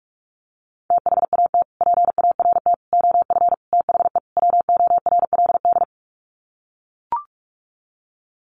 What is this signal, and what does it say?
Morse code, heard as a keyed tone, about 44 wpm, 700 Hz. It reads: THAT PART OF THE WORLD
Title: Sets of 5 Words Encore2 45wpm